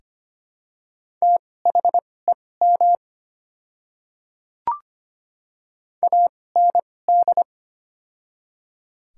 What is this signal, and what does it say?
Morse code, heard as a keyed tone, about 25 wpm, spaced as for 18 wpm, 700 Hz. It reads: THEM AND
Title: 2x Character Spacing - Top 200 Words - 25wpm